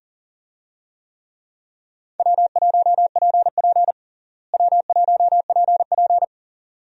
Morse code, heard as a keyed tone, about 40 wpm, 700 Hz. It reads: W1PP W1PP
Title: USA Call Signs - 4 Characters - Repeated 2x - 40wpm